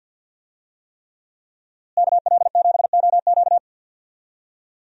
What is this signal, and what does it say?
Morse code, heard as a keyed tone, about 51 wpm, 700 Hz. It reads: KL7OQ
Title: DXCC-100 Call Signs - 5 Characters - Repeated 3x - 50wpm